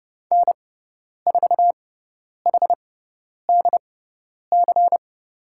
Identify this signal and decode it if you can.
Morse code, heard as a keyed tone, about 30 wpm, spaced as for 10 wpm, 700 Hz. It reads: N4HDC
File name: Lesson 16g - Letter c - Call Signs 30wpm